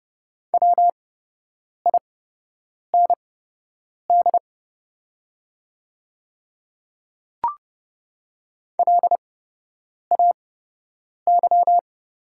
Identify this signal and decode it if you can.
Morse code, heard as a keyed tone, about 30 wpm, spaced as for 8 wpm, 700 Hz. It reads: WIND LAY
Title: 8x Character Spacing - Top 500 Words - 30wpm